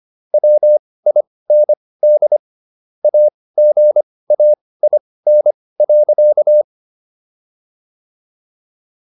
Morse code, heard as a keyed tone, about 25 wpm, spaced as for 18 wpm, 600 Hz. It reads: WIND AGAIN.